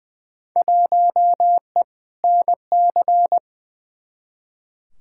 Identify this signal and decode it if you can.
Morse code, heard as a keyed tone, about 20 wpm, 700 Hz. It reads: 1E NC
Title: Field Day Exchanges - 20wpm